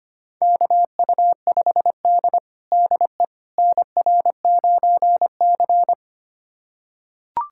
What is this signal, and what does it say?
Morse code, heard as a keyed tone, about 25 wpm, 700 Hz. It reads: KU5D DE NR9C
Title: USA Callsign DE Callsign - 4 Characters - 25wpm